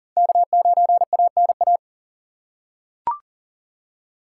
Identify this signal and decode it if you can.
Morse code, heard as a keyed tone, about 40 wpm, 700 Hz. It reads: K9ANA
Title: USA Call Signs - 5 Characters - 4x Character Spacing - 40wpm